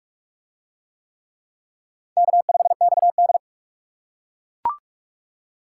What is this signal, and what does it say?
Morse code, heard as a keyed tone, about 44 wpm, 700 Hz. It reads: K5XD